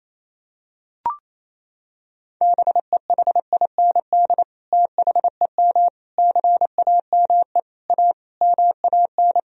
Morse code, heard as a keyed tone, about 27 wpm, 700 Hz. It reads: BEHIND THEM CAME A MAN
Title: Sentences from Top 1000 Words - Review Part 4 - 28wpm